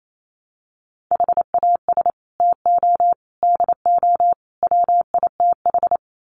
Morse code, heard as a keyed tone, about 28 wpm, 700 Hz. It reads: HAS TO DO WITH
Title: Sets of 4 Words 28wpm